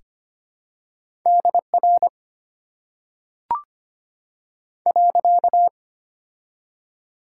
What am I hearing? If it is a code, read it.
Morse code, heard as a keyed tone, about 25 wpm, 700 Hz. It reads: DR .
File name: Top 90 QSO Elements - 6x Character Spacing - 25wpm